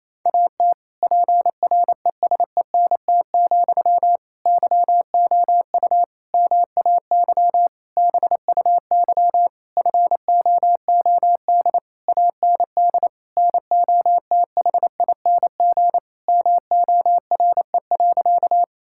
Morse code, heard as a keyed tone, about 27 wpm, 700 Hz. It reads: AT PRESENT, YOU MAY BUY FOOD AND NOTHING MORE.
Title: Sentences from Top 1000 Words - Review Part 5 - 28wpm